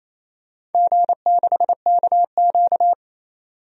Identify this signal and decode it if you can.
Morse code, heard as a keyed tone, about 28 wpm, 700 Hz. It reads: G6KQ
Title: DXCC-100 Call Signs - 4 Characters - 28wpm